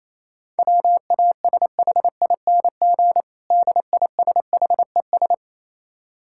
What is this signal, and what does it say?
Morse code, heard as a keyed tone, about 28 wpm, 700 Hz. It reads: WASHING DISHES
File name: Easy Words - Speed-Racing with Hints - 22wpm